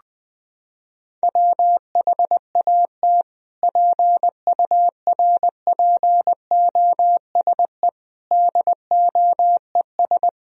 Morse code, heard as a keyed tone, about 21 wpm, 700 Hz. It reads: WHAT PURPOSE DOES
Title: Sentences from Top 1000 Words - Review Part 2 - 20wpm